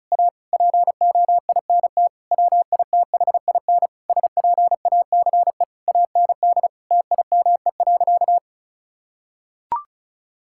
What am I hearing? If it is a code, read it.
Morse code, heard as a keyed tone, about 36 wpm, 700 Hz. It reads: A POINT WITHIN SPACE AND TIME.
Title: Sentences from Top 1000 Words - Review Part 5 - 35wpm